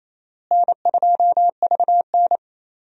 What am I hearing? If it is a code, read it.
Morse code, heard as a keyed tone, about 28 wpm, 700 Hz. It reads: N2VN
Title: USA Call Signs - 4 Characters - Repeated 2x - 28wpm